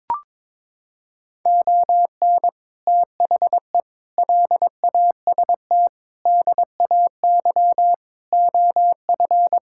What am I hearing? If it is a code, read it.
Morse code, heard as a keyed tone, about 22 wpm, 700 Hz. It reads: ON THE LAST DAY OF
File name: practice / 22wpm / Sets of 6 Words - 22wpm